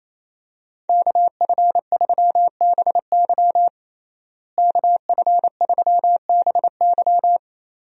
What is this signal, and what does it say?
Morse code, heard as a keyed tone, about 28 wpm, 700 Hz. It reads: KF3BY KF3BY
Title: USA Call Signs - 5 Characters - Repeated 3x - 28wpm